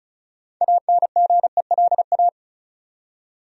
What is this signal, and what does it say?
Morse code, heard as a keyed tone, about 34 wpm, 700 Hz. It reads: ANGELA
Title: USA Common Names - 35wpm